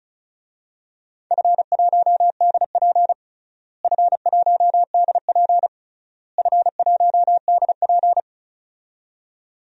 Morse code, heard as a keyed tone, about 36 wpm, 700 Hz. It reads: F1DP F1DP F1DP
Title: DXCC-100 Call Signs - 4 Characters - Repeated 3x - 35wpm